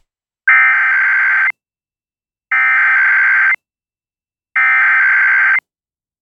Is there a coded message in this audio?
Yes: Emergency Alert System (SAME) tones (header bursts) from 0.5 to 5.6 s, heard as harsh buzzing bursts.